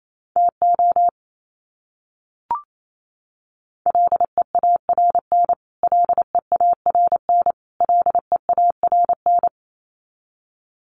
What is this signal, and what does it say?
Morse code, heard as a keyed tone, about 28 wpm, 700 Hz. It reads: TO LEARN LEARN LEARN
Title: Top 500 Words Repeated 28wpm